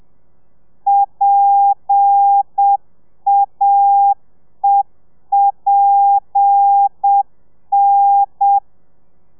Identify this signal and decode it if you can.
Morse code, heard as a keyed tone, about 7 wpm, 800 Hz. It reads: PAEPN
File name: telegrafia / i2tfj / Lez7 30